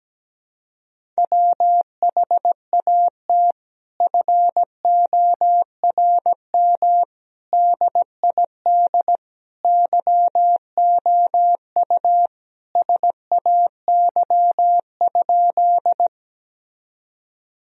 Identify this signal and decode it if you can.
Morse code, heard as a keyed tone, about 17 wpm, 700 Hz. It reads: WHAT FORM DID YOU SAY?
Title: Sentences from Top 1000 Words - Review Part 1 - 17wpm